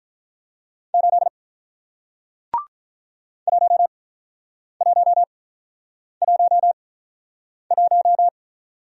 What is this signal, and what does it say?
Morse code, heard as a keyed tone, about 48 wpm, spaced as for 9 wpm, 690 Hz. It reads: 8 1111